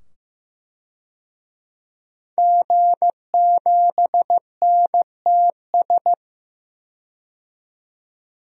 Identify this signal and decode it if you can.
Morse code, heard as a keyed tone, about 15 wpm, 700 Hz. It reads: G7NTS
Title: DXCC-100 Call Signs - 5 Characters - 15wpm